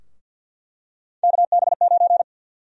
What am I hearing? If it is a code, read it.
Morse code, heard as a keyed tone, about 49 wpm, 700 Hz. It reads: KB9
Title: USA Call Sign Prefixes - 50wpm